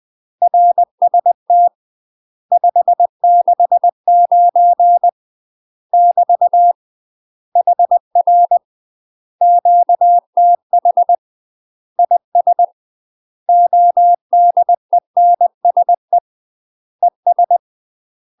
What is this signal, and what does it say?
Morse code, heard as a keyed tone, about 20 wpm, 700 Hz. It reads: RST 569 <BT> HR QTH IS ODENSE ES
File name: UK MASTER QSOs TRAINING / 20WPM / UK MASTER QSOs 0070